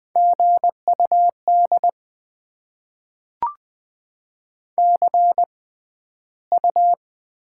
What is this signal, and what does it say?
Morse code, heard as a keyed tone, about 20 wpm, spaced as for 12 wpm, 700 Hz. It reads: GUD CU
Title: Top 20 QSO Elements - 6x Character Spacing - 20wpm